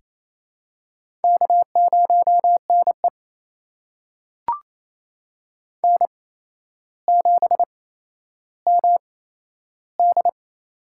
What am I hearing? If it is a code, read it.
Morse code, heard as a keyed tone, about 28 wpm, spaced as for 12 wpm, 700 Hz. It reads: K0NE N7MD
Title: USA Call Signs - 4 Characters - 8x Character Spacing - 28wpm